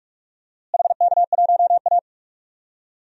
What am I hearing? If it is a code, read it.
Morse code, heard as a keyed tone, about 44 wpm, 700 Hz. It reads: HK1A
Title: DXCC-100 Call Signs - 4 Characters - Repeated 2x - 45wpm